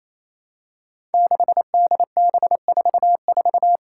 Morse code, heard as a keyed tone, about 29 wpm, 700 Hz. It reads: 6DB44
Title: Five Challenging Letter-Number Combinations 28wpm